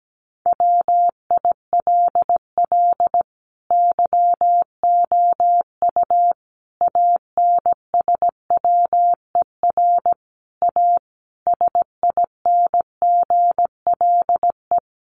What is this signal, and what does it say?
Morse code, heard as a keyed tone, about 17 wpm, 700 Hz. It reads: WILL YOU ANSWER A SINGLE
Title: Sentences from Top 700 Words 17wpm